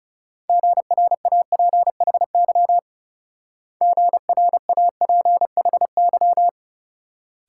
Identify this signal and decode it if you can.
Morse code, heard as a keyed tone, about 38 wpm, 700 Hz. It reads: GRAPHY GRAPHY